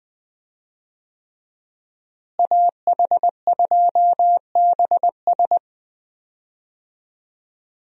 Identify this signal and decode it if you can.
Morse code, heard as a keyed tone, about 20 wpm, 700 Hz. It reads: AH2BS
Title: USA Call Signs - 5 Characters - Repeated 3x - 20wpm